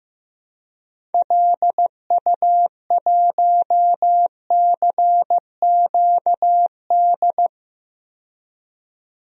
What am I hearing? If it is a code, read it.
Morse code, heard as a keyed tone, about 15 wpm, 700 Hz. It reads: LU1CQD
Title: DXCC-100 Call Signs - 6 Characters - 15wpm